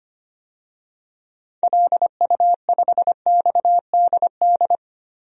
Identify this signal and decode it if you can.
Morse code, heard as a keyed tone, about 25 wpm, 700 Hz. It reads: LU5XDD
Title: DXCC-100 Call Signs - 6 Characters - 25wpm